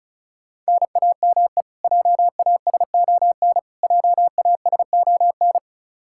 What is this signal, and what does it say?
Morse code, heard as a keyed tone, about 34 wpm, 700 Hz. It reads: NAME JASON JASON
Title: Practice QSO - Exchange 1 - Beginning - 35wpm